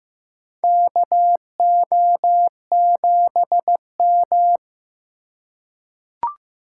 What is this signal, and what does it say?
Morse code, heard as a keyed tone, about 15 wpm, 700 Hz. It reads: KO7M
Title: USA Call Signs - 4 Characters - 3x Character Spacing - 15wpm